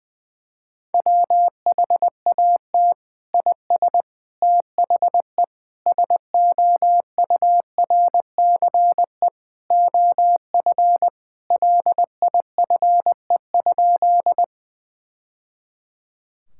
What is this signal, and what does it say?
Morse code, heard as a keyed tone, about 20 wpm, 700 Hz. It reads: WHAT IS THE SOURCE OF LIFE?
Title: Sentences from Top 1000 Words - Review Part 5 - 20wpm